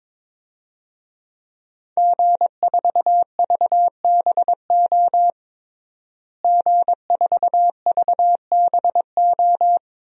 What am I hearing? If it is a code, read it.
Morse code, heard as a keyed tone, about 22 wpm, 700 Hz. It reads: G4VBO G4VBO
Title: DXCC-100 Call Signs - 5 Characters - Repeated 3x - 22wpm